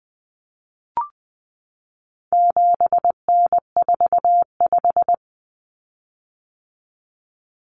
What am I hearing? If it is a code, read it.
Morse code, heard as a keyed tone, about 20 wpm, 700 Hz. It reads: 7N45
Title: Common Cut Numbers - 4-Digit 20wpm